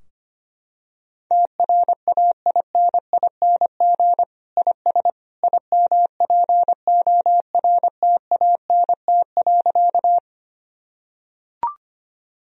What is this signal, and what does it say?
Morse code, heard as a keyed tone, about 25 wpm, 700 Hz. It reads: TRAINING IS IMPORTANT.
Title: Sentences from Top 1000 Words - Review Part 4 - 25wpm